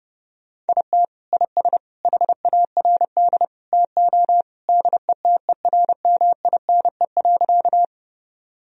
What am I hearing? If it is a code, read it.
Morse code, heard as a keyed tone, about 30 wpm, 700 Hz. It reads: IT IS HARD TO DETERMINE.
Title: Sentences from Top 1000 Words - Review Part 4 - 30wpm